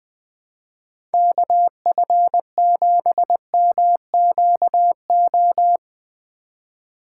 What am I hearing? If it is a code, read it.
Morse code, heard as a keyed tone, about 20 wpm, 700 Hz. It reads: KF7MQO
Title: USA Call Signs - 6 Characters - 2x Character Spacing - 20wpm